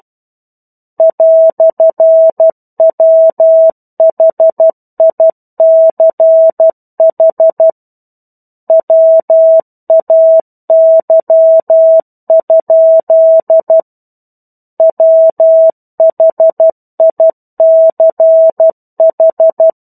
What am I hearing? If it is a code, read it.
Morse code, heard as a keyed tone, about 12 wpm, 650 Hz. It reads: "WHICH WAY? WHICH
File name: 01-Chapter-01 12 WPM